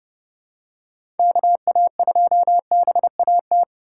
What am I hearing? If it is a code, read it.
Morse code, heard as a keyed tone, about 30 wpm, 700 Hz. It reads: KA2BAT
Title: USA Call Signs - 6 Characters - 3x Character Spacing - 30wpm